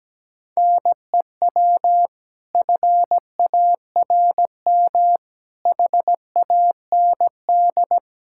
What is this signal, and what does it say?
Morse code, heard as a keyed tone, about 17 wpm, 700 Hz. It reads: NEW FARM HAND
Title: Pangram Challenge v2 - 17wpm